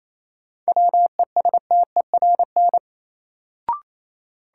Morse code, heard as a keyed tone, about 28 wpm, 700 Hz. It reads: WESTERN
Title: Sentences from Top 1000 Words - Review Part 7 - New Word Drill - 28wpm